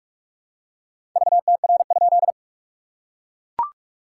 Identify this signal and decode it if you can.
Morse code, heard as a keyed tone, about 44 wpm, 690 Hz. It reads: VTR?
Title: SOTA QSO Practice 45wpm